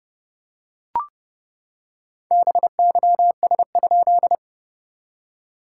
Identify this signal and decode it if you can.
Morse code, heard as a keyed tone, about 30 wpm, 690 Hz. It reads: BYS?
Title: SOTA QSO Practice 30wpm